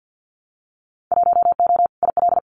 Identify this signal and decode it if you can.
Morse code, heard as a keyed tone, about 50 wpm, 700 Hz. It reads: 1O IL
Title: Winter Field Day Exchanges 50wpm